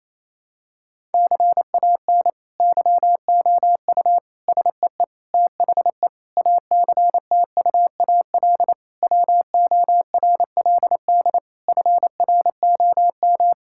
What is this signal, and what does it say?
Morse code, heard as a keyed tone, about 28 wpm, 700 Hz. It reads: CAN YOU SEE THE ACTUAL WORLD FROM
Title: Sentences from Top 1000 Words - Review Part 5 - 28wpm